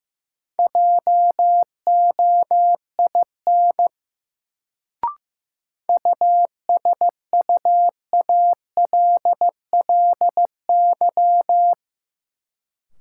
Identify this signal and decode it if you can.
Morse code, heard as a keyed tone, about 15 wpm, 700 Hz. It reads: JOIN USUALLY
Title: Sentences from Top 1400 Words - New Word Drill Alt - 15wpm